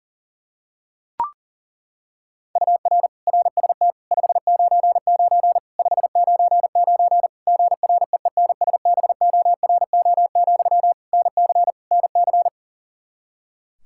Morse code, heard as a keyed tone, about 40 wpm, 700 Hz. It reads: UR RST 599 599 GREENSBORO, NC NC
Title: Practice QSO - Exchange 1 - Beginning - 40wpm